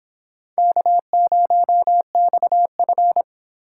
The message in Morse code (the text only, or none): K0XF